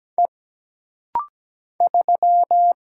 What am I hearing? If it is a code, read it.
Morse code, heard as a keyed tone, about 17 wpm, 700 Hz. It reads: E 3
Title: Single Letter-Number Alt - Rapid-Fire 17wpm